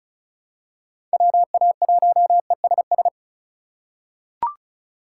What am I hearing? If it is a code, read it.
Morse code, heard as a keyed tone, about 36 wpm, 700 Hz. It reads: WA1ESS
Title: USA Call Signs - 6 Characters - 2x Character Spacing - 35wpm